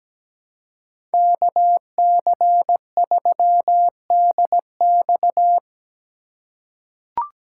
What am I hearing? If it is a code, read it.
Morse code, heard as a keyed tone, about 17 wpm, 700 Hz. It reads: KC3DX
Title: USA Call Signs - 4x Character Spacing - 17wpm